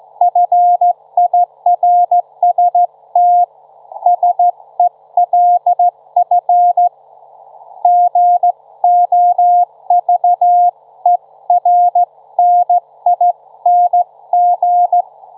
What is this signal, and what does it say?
Morse code, heard as a keyed tone, about 15 wpm, 710 Hz. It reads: FIRST SELF GOVERNING
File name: Trivia-ZL2WT